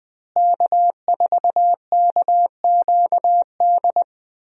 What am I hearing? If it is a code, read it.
Morse code, heard as a keyed tone, about 20 wpm, 700 Hz. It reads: K4KQD
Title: USA Call Signs - 5 Characters - Repeated 3x - 20wpm